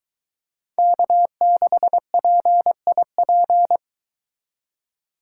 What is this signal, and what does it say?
Morse code, heard as a keyed tone, about 23 wpm, 700 Hz. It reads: K6PIP